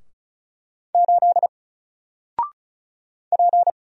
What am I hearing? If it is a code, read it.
Morse code, heard as a keyed tone, about 35 wpm, 700 Hz. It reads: 8 P
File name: Single-Letter-Number-35wpm